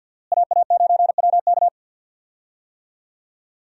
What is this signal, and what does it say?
Morse code, heard as a keyed tone, about 52 wpm, 700 Hz. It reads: AA9WK
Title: USA Call Signs - 5 Characters - Repeated 3x - 50wpm